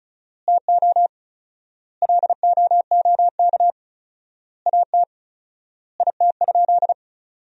Morse code, heard as a keyed tone, about 36 wpm, 700 Hz. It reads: TO LOOK AT IT?
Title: 4x Word-Spacing - Sentences from Top 100 Words - 35wpm